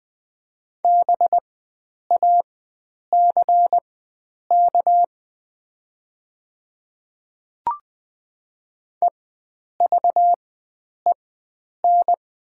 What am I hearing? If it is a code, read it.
Morse code, heard as a keyed tone, about 20 wpm, spaced as for 9 wpm, 700 Hz. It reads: BACK EVEN
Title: 4x Character Spacing - Top 300 Words - 20wpm